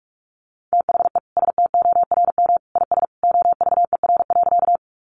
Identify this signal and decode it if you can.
Morse code, heard as a keyed tone, about 44 wpm, 700 Hz. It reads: THE STORM IS OVER.
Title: Sentences from Top 1500 Words 45wpm